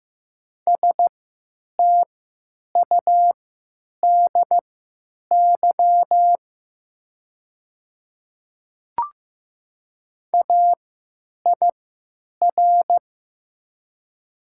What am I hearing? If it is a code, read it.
Morse code, heard as a keyed tone, about 15 wpm, spaced as for 8 wpm, 700 Hz. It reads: STUDY AIR
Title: 3x Character Spacing - Top 300 Words - 15wpm